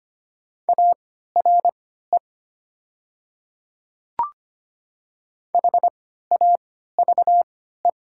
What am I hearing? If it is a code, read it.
Morse code, heard as a keyed tone, about 25 wpm, spaced as for 14 wpm, 700 Hz. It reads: ARE HAVE
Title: 3x Character Spacing - Top 100 Words - 25wpm